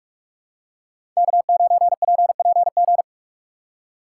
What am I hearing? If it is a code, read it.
Morse code, heard as a keyed tone, about 44 wpm, 700 Hz. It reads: K9PPG
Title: USA Call Signs - 5 Characters - 2x Character Spacing - 45wpm